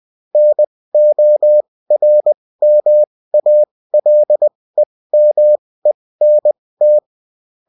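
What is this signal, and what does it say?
Morse code, heard as a keyed tone, about 20 wpm, spaced as for 16 wpm, 600 Hz. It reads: NORMALEMENT
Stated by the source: Bulletin in text and CW form.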